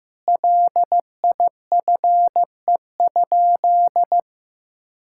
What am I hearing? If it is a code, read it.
Morse code, heard as a keyed tone, about 15 wpm, 700 Hz. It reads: LIFE?
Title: Sentences from Top 1600 Words 15wpm